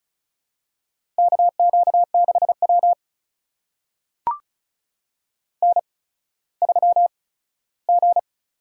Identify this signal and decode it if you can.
Morse code, heard as a keyed tone, about 36 wpm, spaced as for 17 wpm, 700 Hz. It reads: KQ6W N3G
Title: USA Call Signs - 4 Characters - 8x Character Spacing - 35wpm